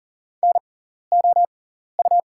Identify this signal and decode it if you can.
Morse code, heard as a keyed tone, about 39 wpm, spaced as for 14 wpm, 690 Hz. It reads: NOU
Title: 6x Character Spacing - Top 500 Words - 40wpm